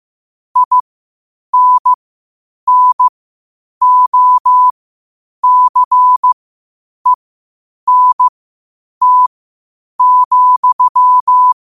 Morse code, heard as a keyed tone, about 15 wpm, spaced as for 8 wpm, 1000 Hz. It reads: INNOCENT,
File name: qotdfiles / 8wpm